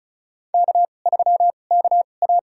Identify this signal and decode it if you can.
Morse code, heard as a keyed tone, about 36 wpm, spaced as for 25 wpm, 700 Hz. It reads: K3KA
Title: USA Call Signs - 2x Character Spacing - 35wpm